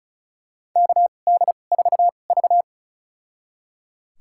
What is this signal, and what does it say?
Morse code, heard as a keyed tone, about 34 wpm, spaced as for 25 wpm, 700 Hz. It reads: KD4V
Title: USA Call Signs - 4 Characters - 2x Character Spacing - 35wpm